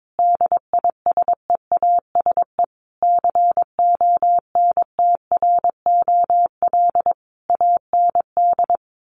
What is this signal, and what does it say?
Morse code, heard as a keyed tone, about 22 wpm, 700 Hz. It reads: DISEASE CONTROL AND
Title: Sets of 5 Words Encore2 22wpm